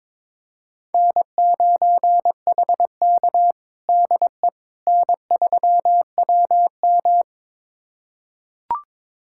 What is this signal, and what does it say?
Morse code, heard as a keyed tone, about 22 wpm, 700 Hz. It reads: N9HK DE N3WM
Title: USA Callsign DE Callsign - 4 Characters - 22wpm